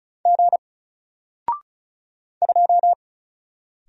Morse code, heard as a keyed tone, about 35 wpm, 700 Hz. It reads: G 2
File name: Single-Letter-Number-35wpm